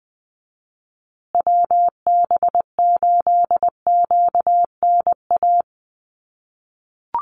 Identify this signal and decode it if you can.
Morse code, heard as a keyed tone, about 20 wpm, 700 Hz. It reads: WB8QNA
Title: USA Call Signs Repeated 3x 20wpm